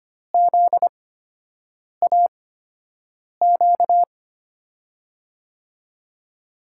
Morse code, heard as a keyed tone, about 25 wpm, spaced as for 7 wpm, 700 Hz. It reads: ZAQ
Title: USA Call Signs - 6 Characters - 8x Character Spacing - 25wpm